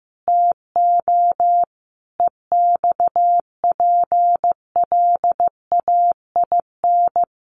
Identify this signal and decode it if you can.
Morse code, heard as a keyed tone, about 15 wpm, 700 Hz. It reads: TO EXPLAIN
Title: Sentences from Top 1100 Words 15wpm